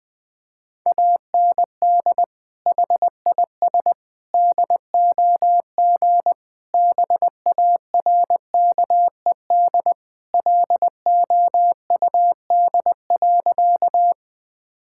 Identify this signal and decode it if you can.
Morse code, heard as a keyed tone, about 21 wpm, 700 Hz. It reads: AND HIS DOG BARKED LOUD.
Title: Sentences with 2-Word Phrases - Spoken Beginning - 20wpm